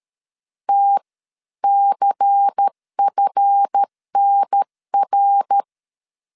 Morse code, heard as a keyed tone, about 13 wpm, spaced as for 11 wpm, 800 Hz. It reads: TCFNR
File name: LEZIONI / 9 28 12wpm 2351